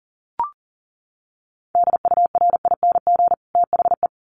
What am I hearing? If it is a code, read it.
Morse code, heard as a keyed tone, about 40 wpm, 700 Hz. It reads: DURING THE
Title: Sets of 2 Words 40wpm